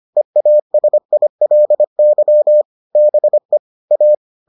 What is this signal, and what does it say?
Morse code, heard as a keyed tone, about 25 wpm, 600 Hz. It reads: EASILY BE A